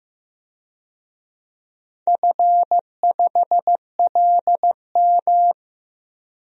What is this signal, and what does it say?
Morse code, heard as a keyed tone, about 15 wpm, 700 Hz. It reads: F5LM
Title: DXCC-100 Call Signs - 4 Characters - 15wpm